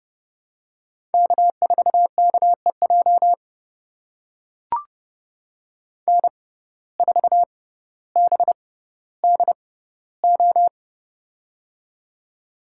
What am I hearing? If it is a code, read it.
Morse code, heard as a keyed tone, about 30 wpm, spaced as for 15 wpm, 700 Hz. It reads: K4KEJ N4BDO
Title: USA Call Signs - 5 Characters - 6x Character Spacing - 30wpm